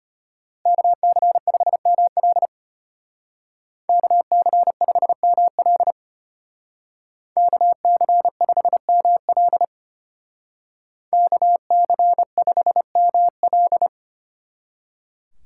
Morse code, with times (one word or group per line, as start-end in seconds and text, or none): 0.7-2.5 KC5ML
3.9-5.9 KC5ML
7.4-9.6 KC5ML
11.1-13.9 KC5ML